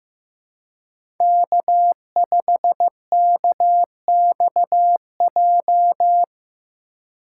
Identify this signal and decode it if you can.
Morse code, heard as a keyed tone, about 15 wpm, 700 Hz. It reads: K5KXJ
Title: K1USN SST QSO Practice - 15wpm